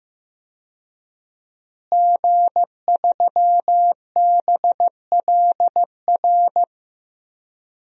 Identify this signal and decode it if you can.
Morse code, heard as a keyed tone, about 15 wpm, 700 Hz. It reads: G3BLR
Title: DXCC-100 Call Signs - 5 Characters - Repeated 3x - 15wpm